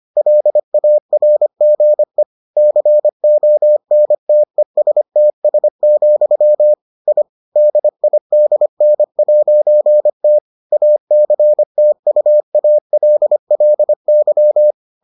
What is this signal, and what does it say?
Morse code, heard as a keyed tone, about 25 wpm, 600 Hz. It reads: LARGE CONTESTS, I DIDN'T ACTUALLY